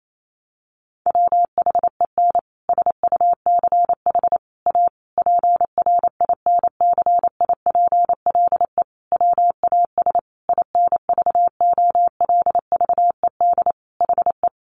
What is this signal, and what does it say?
Morse code, heard as a keyed tone, about 28 wpm, 700 Hz. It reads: WHEN SUCH A PRINCIPLE WAS INVOLVED HE
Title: Sentences from Top 1400 Words 28wpm